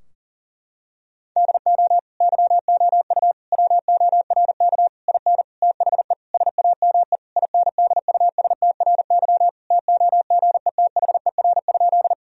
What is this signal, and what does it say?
Morse code, heard as a keyed tone, about 40 wpm, 700 Hz. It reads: DO YOU WORK IN THE SAME INDUSTRY TOGETHER?
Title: Sentences from Top 1000 Words - Review Part 4 - 40wpm